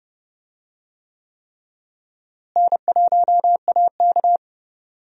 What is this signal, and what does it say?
Morse code, heard as a keyed tone, about 30 wpm, 700 Hz. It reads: N1AK
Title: USA Call Signs - 4 Characters - Repeated 3x - 30wpm